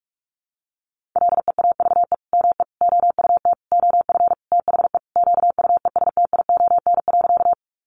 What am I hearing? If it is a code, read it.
Morse code, heard as a keyed tone, about 46 wpm, 700 Hz. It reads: LEAVE ME OUT OF THE QUESTION.
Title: Sentences from Top 400 Words 45wpm